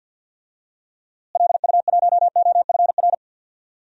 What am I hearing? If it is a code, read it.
Morse code, heard as a keyed tone, about 51 wpm, 700 Hz. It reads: LU1OFR
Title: DXCC-100 Call Signs - 6 Characters - Repeated 3x - 50wpm